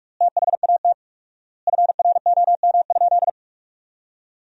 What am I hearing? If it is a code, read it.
Morse code, heard as a keyed tone, about 46 wpm, 700 Hz. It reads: THAT FROM?
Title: 4x Word-Spacing - Sentences from Top 300 Words - 45wpm